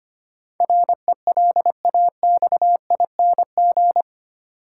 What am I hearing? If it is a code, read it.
Morse code, heard as a keyed tone, about 25 wpm, 700 Hz. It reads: RELAXING
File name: Eight Letter Words - 25wpm